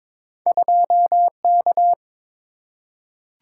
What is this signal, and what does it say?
Morse code, heard as a keyed tone, about 22 wpm, 690 Hz. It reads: 2K
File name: practice / 22wpm / Two Letter-Number Combinations - 22wpm